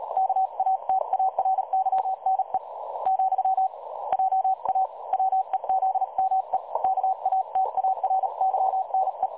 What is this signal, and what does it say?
Morse code, heard as a keyed tone, about 38 wpm, 760 Hz. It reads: KATARYNE , JA WEZME KATARYNE
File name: dlazaawansowanych / nagraniazpasma / websdr recording 3565.0kHz